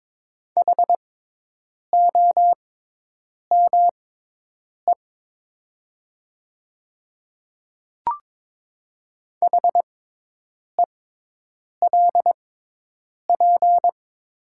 Morse code, heard as a keyed tone, about 22 wpm, spaced as for 8 wpm, 700 Hz. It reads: HOME HELP
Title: 6x Character Spacing - Top 200 Words - 22wpm